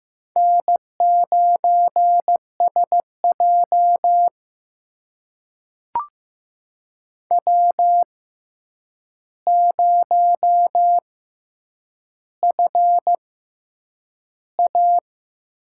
Morse code, heard as a keyed tone, about 15 wpm, spaced as for 8 wpm, 700 Hz. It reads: N9SJ W0FA
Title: USA Call Signs - 6x Character Spacing - 15wpm